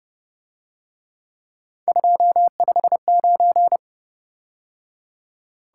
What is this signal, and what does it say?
Morse code, heard as a keyed tone, about 29 wpm, 700 Hz. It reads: 259
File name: Speed-Racing - Three Letter-Number Combinations - 25wpm